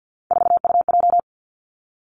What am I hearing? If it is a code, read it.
Morse code, heard as a keyed tone, about 50 wpm, 700 Hz. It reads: 4UJ
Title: Three Challenging Letter-Number Combinations 50wpm